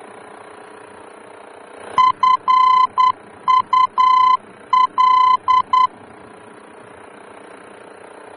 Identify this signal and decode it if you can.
Morse code, heard as a keyed tone, about 10 wpm, 1020 Hz. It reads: FUL
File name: germany / mx / FUL-D-VOR